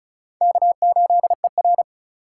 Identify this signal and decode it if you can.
Morse code, heard as a keyed tone, about 34 wpm, 700 Hz. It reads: K8ER
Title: USA Call Signs - 4 Characters - 2x Character Spacing - 35wpm